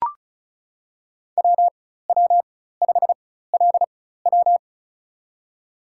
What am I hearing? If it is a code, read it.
Morse code, heard as a keyed tone, about 36 wpm, spaced as for 16 wpm, 700 Hz. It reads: WW5LW
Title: USA Call Signs - 5 Characters - 4x Character Spacing - 35wpm